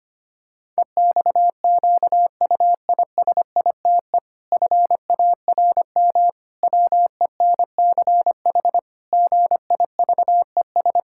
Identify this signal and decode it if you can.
Morse code, heard as a keyed tone, about 25 wpm, 700 Hz. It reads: EXQUISITE FARM WENCH GIVES
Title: Pangram Challenge v2 - 25wpm